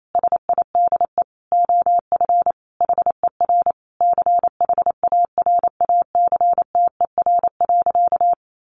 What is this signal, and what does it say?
Morse code, heard as a keyed tone, about 28 wpm, 700 Hz. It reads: SIDE OF HER CHARACTER.
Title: Sentences from Top 1200 Words 28wpm